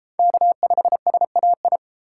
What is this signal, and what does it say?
Morse code, heard as a keyed tone, about 29 wpm, 700 Hz. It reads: K5SAI